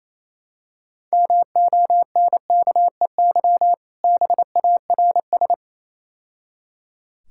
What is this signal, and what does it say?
Morse code, heard as a keyed tone, about 27 wpm, 700 Hz. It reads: MONKEY BARS
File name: Easy Words - Speed-Racing with Hints - 22wpm